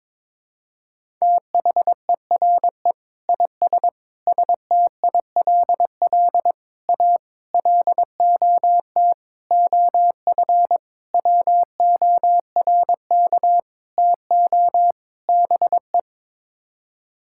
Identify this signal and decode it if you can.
Morse code, heard as a keyed tone, about 22 wpm, 700 Hz. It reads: THERE IS STILL A LOT OF WORK TO BE
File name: Sets of 9 Words - 22wpm